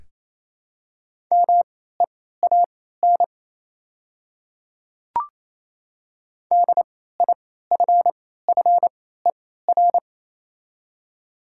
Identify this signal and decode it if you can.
Morse code, heard as a keyed tone, about 27 wpm, spaced as for 16 wpm, 700 Hz. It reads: MEAN DIFFER
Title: 3x Character Spacing - Top 200 Words - 28wpm